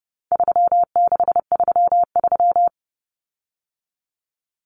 Morse code, heard as a keyed tone, about 30 wpm, 700 Hz. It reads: 3633
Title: Common Cut Numbers - 4-Digit 30wpm